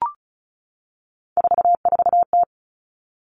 Morse code, heard as a keyed tone, about 36 wpm, 700 Hz. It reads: 44T
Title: Three Challenging Letter-Number Combinations 35wpm